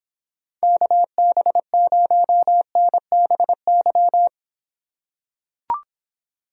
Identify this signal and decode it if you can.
Morse code, heard as a keyed tone, about 26 wpm, 700 Hz. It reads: KB0NBY